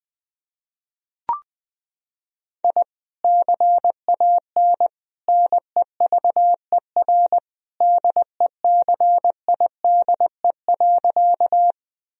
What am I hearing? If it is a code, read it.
Morse code, heard as a keyed tone, about 20 wpm, 700 Hz. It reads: I CAN NEVER DECIDE.
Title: Sentences from Top 1000 Words - Review Part 3 - 20wpm